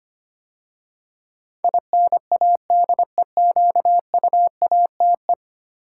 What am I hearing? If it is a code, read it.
Morse code, heard as a keyed tone, about 25 wpm, 700 Hz. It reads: INADEQUATE
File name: 10 Letter Words - First 3 Letters Given - 25wpm